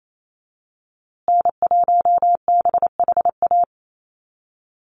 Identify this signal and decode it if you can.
Morse code, heard as a keyed tone, about 28 wpm, 700 Hz. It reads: N1BHA
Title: USA Call Signs Repeated 3x 28wpm